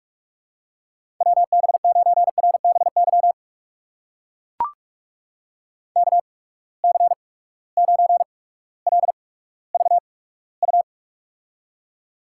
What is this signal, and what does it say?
Morse code, heard as a keyed tone, about 41 wpm, spaced as for 19 wpm, 700 Hz. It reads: WB9RBY KC9LVU
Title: USA Call Signs - 6 Characters - 8x Character Spacing - 45wpm